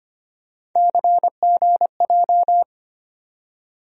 Morse code, heard as a keyed tone, about 25 wpm, 700 Hz. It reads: CGJ
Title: USA Call Sign Postfixes - 3 Letters 25wpm